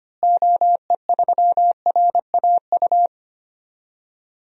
Morse code, heard as a keyed tone, about 25 wpm, 700 Hz. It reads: OE3RAU
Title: DXCC-100 Call Signs - 6 Characters - Repeated 3x - 25wpm